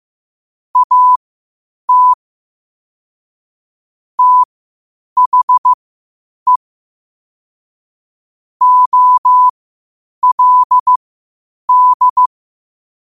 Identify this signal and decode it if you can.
Morse code, heard as a keyed tone, about 15 wpm, spaced as for 8 wpm, 1000 Hz. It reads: AT THE OLD
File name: qotdfiles / 8wpm